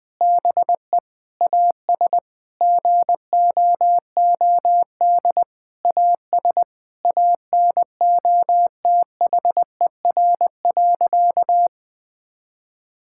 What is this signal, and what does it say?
Morse code, heard as a keyed tone, about 20 wpm, 700 Hz. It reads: BE AS GOOD AS ANOTHER.
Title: Sentences from Top 1600 Words 20wpm